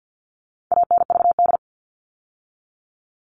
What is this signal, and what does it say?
Morse code, heard as a keyed tone, about 50 wpm, 700 Hz. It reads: ANVD
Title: All Cut Numbers - 4-Digit 50wpm